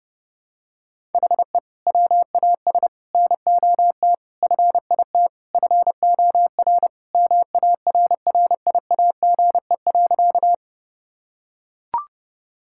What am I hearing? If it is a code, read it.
Morse code, heard as a keyed tone, about 30 wpm, 700 Hz. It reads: HE WAS NOT FIT FOR MARRIAGE.
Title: Sentences from Top 1000 Words - Review Part 5 - 30wpm